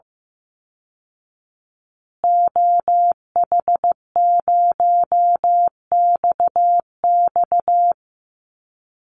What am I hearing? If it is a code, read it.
Morse code, heard as a keyed tone, about 15 wpm, 700 Hz. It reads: OH0XX
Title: Call Signs Repeated 3x - 15wpm